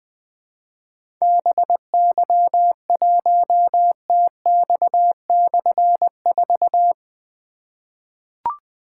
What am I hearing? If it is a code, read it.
Morse code, heard as a keyed tone, about 20 wpm, 700 Hz. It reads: BY1TX/4
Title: Difficult Call Signs 20wpm